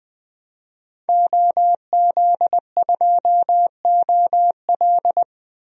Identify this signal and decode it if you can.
Morse code, heard as a keyed tone, about 20 wpm, 700 Hz. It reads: OZ2OL
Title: DXCC-100 Call Signs - 5 Characters - 20wpm